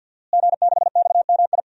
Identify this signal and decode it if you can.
Morse code, heard as a keyed tone, about 52 wpm, 700 Hz. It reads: G6XMI
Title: DXCC-100 Call Signs - 5 Characters - Repeated 2x - 50wpm